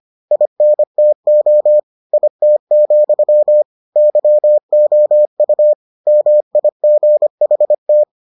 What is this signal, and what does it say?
Morse code, heard as a keyed tone, about 25 wpm, 600 Hz. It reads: INTO IT, YOU MIGHT